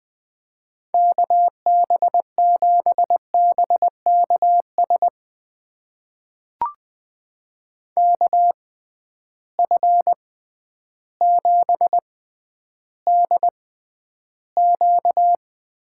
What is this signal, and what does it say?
Morse code, heard as a keyed tone, about 21 wpm, spaced as for 11 wpm, 700 Hz. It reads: KB7BKS KF7DQ
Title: USA Call Signs - 6 Characters - 6x Character Spacing - 20wpm